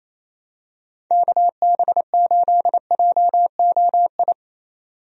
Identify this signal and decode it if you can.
Morse code, heard as a keyed tone, about 27 wpm, 700 Hz. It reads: KB8JOI
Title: USA Call Signs - 6 Characters - 28wpm